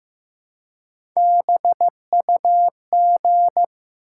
Morse code, heard as a keyed tone, about 15 wpm, 700 Hz. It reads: BUG
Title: Three Letter Words - 15wpm